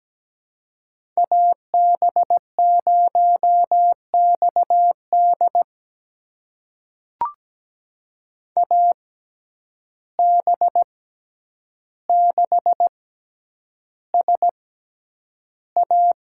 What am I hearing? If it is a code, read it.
Morse code, heard as a keyed tone, about 17 wpm, spaced as for 9 wpm, 700 Hz. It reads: AB0XD AB6SA
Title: USA Call Signs - 5 Characters - 6x Character Spacing - 17wpm